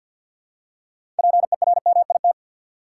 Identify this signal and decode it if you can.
Morse code, heard as a keyed tone, about 48 wpm, 700 Hz. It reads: PERMIT
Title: Six Letter Words - 50wpm